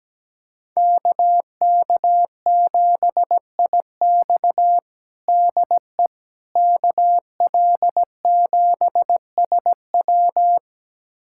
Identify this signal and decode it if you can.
Morse code, heard as a keyed tone, about 17 wpm, 700 Hz. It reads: KK7IX DE KL7SW
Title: USA Callsign DE Callsign - 5 Characters - 17wpm